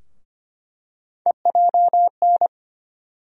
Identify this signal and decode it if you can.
Morse code, heard as a keyed tone, about 25 wpm, 700 Hz. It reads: EJN
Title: USA Call Sign Postfixes - 3 Letters 25wpm